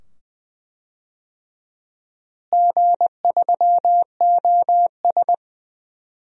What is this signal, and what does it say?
Morse code, heard as a keyed tone, about 20 wpm, 700 Hz. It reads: G3OS
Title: DXCC-100 Call Signs - 4 Characters - Repeated 2x - 20wpm